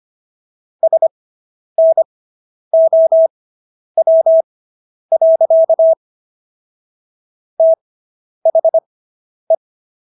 Morse code, heard as a keyed tone, about 25 wpm, spaced as for 10 wpm, 650 Hz. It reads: SNOW. THE